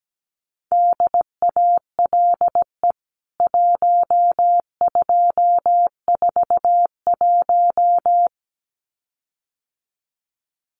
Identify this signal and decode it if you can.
Morse code, heard as a keyed tone, about 17 wpm, 700 Hz. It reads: DALE 1241
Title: CWT Exchange 17wpm